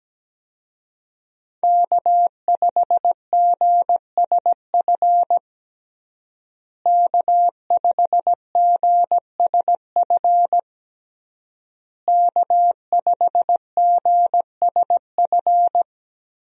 Morse code, heard as a keyed tone, about 17 wpm, 700 Hz. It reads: K5GSF K5GSF K5GSF
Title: USA Call Signs - 5 Characters - Repeated 3x - 17wpm